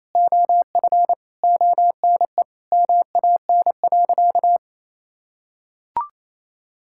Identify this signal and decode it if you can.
Morse code, heard as a keyed tone, about 28 wpm, 700 Hz. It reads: OF ONE MAN.
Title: Sentences from Top 1000 Words - Review Part 6 - 28wpm